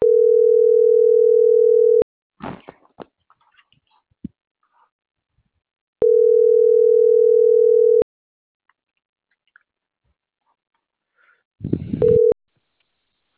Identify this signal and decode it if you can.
Ringback tone (North America), 440 + 480 Hz.